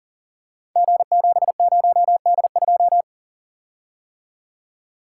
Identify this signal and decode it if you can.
Morse code, heard as a keyed tone, about 41 wpm, 700 Hz. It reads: G70D2
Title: Code Groups - 5 Letter:Numbers - 40wpm